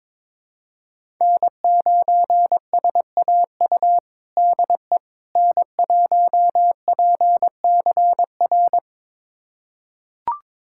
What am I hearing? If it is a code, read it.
Morse code, heard as a keyed tone, about 22 wpm, 700 Hz. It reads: N9SAU DE N1PCR
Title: USA Callsign DE Callsign - 5 Characters - 22wpm